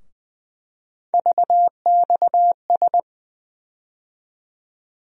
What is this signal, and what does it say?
Morse code, heard as a keyed tone, about 20 wpm, 700 Hz. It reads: VXS
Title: USA Call Sign Postfixes - 3 Letters 20wpm